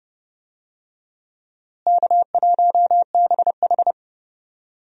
Morse code, heard as a keyed tone, about 30 wpm, 700 Hz. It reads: K1BH
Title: USA Call Signs - 4 Characters - Repeated 3x - 30wpm